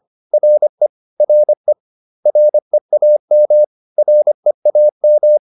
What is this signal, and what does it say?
Morse code, heard as a keyed tone, about 25 wpm, 600 Hz. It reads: RE RE REAM REAM